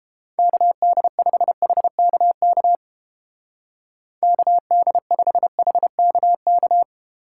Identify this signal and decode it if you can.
Morse code, heard as a keyed tone, about 30 wpm, 700 Hz. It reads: KD5HKK KD5HKK